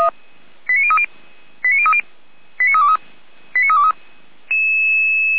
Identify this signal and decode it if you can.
Five-tone selective call (ZVEI), heard as quick runs of five tones, tones of about 70 ms. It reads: 89030 89030 89323 89323